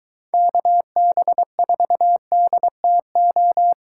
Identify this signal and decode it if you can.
Morse code, heard as a keyed tone, about 23 wpm, 700 Hz. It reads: KB4DTO